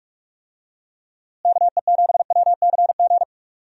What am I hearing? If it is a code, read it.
Morse code, heard as a keyed tone, about 46 wpm, 700 Hz. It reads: KE7WCG
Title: USA Call Signs - 6 Characters - Repeated 3x - 45wpm